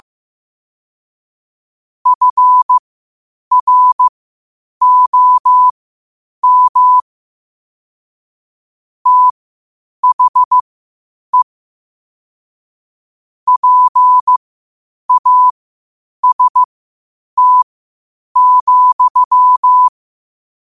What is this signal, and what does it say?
Morse code, heard as a keyed tone, about 15 wpm, spaced as for 8 wpm, 1000 Hz. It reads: FROM THE PAST,
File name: qotdfiles / 8wpm